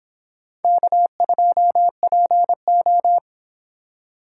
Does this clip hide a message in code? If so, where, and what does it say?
Yes: Morse code from 0.6 to 3.2 s, heard as a keyed tone, about 24 wpm, 700 Hz. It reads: K2PO